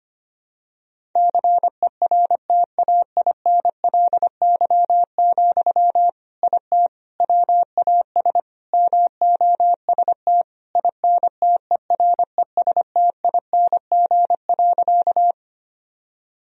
Morse code, heard as a keyed tone, about 25 wpm, 700 Hz. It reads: CERTAINLY, IT WAS MOST INTERESTING.
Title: Sentences from Top 1000 Words - Review Part 1 - 25wpm